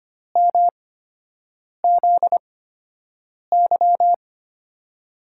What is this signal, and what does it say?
Morse code, heard as a keyed tone, about 25 wpm, spaced as for 7 wpm, 700 Hz. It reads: MZY
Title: USA Call Signs - 8x Character Spacing - 25wpm